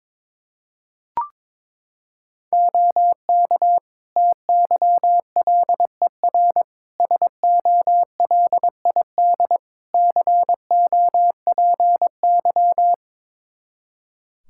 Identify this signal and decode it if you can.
Morse code, heard as a keyed tone, about 22 wpm, 700 Hz. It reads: OK TYLER SOLID COPY
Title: Practice QSO - Exchange 3 - More Info - 22wpm